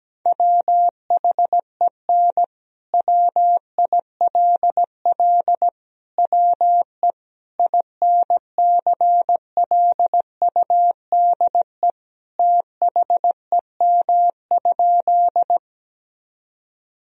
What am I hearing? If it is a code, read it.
Morse code, heard as a keyed tone, about 17 wpm, 700 Hz. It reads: WHEN WILL WE INCLUDE THEM?
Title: Sentences from Top 1000 Words - Review Part 3 - 17wpm